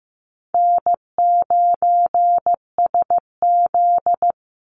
Morse code, heard as a keyed tone, about 15 wpm, 700 Hz. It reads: N9SZ
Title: Easy Call Signs Repeated 3x - 15wpm